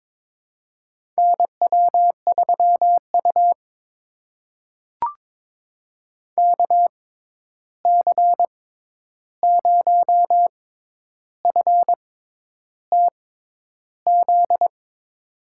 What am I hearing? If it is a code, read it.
Morse code, heard as a keyed tone, about 22 wpm, spaced as for 10 wpm, 700 Hz. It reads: NW3U KC0FTZ
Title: USA Call Signs - 6x Character Spacing - 22wpm